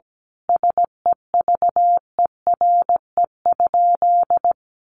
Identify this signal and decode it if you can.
Morse code, heard as a keyed tone, about 17 wpm, 700 Hz. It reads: SEVERE?
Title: Sentences from Top 1400 Words 17wpm